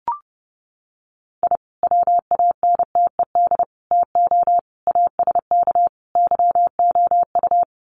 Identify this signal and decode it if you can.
Morse code, heard as a keyed tone, about 30 wpm, 700 Hz. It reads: I WANTED TO ASK YOU
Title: Sets of 5 Words Encore2 30wpm